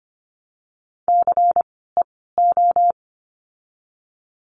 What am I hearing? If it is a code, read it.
Morse code, heard as a keyed tone, about 25 wpm, 700 Hz. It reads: C E O
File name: course / 25:10wpm / 094 Lesson 16f - Letter c - Words 25wpm